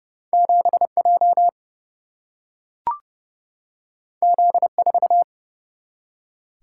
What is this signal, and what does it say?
Morse code, heard as a keyed tone, about 30 wpm, 700 Hz. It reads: 7J Z4
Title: Single-Letter Single-Number 30wpm